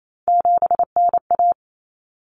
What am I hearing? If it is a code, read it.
Morse code, heard as a keyed tone, about 28 wpm, 700 Hz. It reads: 7NA
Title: Common Cut Numbers - 3-Digit 28wpm